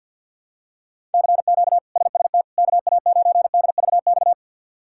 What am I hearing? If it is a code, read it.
Morse code, heard as a keyed tone, about 51 wpm, 700 Hz. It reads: CQ SST KA9DVX
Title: K1USN SST QSO Practice - 50wpm